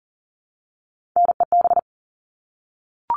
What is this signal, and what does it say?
Morse code, heard as a keyed tone, about 40 wpm, 700 Hz. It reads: NEB
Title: All Cut Numbers - 3-Digit 40wpm